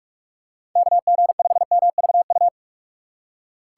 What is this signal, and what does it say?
Morse code, heard as a keyed tone, about 46 wpm, 700 Hz. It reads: KG5MVU
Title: USA Call Signs - 6 Characters - 3x Character Spacing - 45wpm